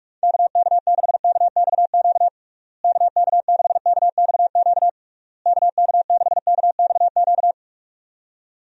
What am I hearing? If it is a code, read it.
Morse code, heard as a keyed tone, about 42 wpm, 700 Hz. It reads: KK6KXQ KK6KXQ KK6KXQ
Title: USA Call Signs - 6 Characters - Repeated 3x - 45wpm